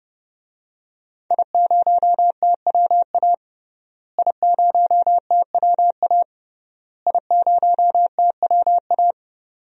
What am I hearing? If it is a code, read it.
Morse code, heard as a keyed tone, about 30 wpm, 700 Hz. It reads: I0TWA I0TWA I0TWA
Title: DXCC-100 Call Signs - 5 Characters - Repeated 3x - 30wpm